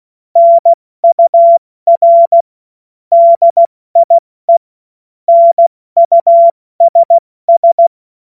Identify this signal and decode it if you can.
Morse code, heard as a keyed tone, about 16 wpm, 680 Hz. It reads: NUR DIE NUSS
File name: Morsekurs / Witze / Witz 05. Oktober 2021, 70 F80